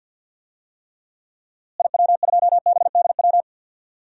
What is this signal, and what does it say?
Morse code, heard as a keyed tone, about 53 wpm, 700 Hz. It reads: IW2BDW
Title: DXCC-100 Call Signs - 6 Characters - Repeated 3x - 50wpm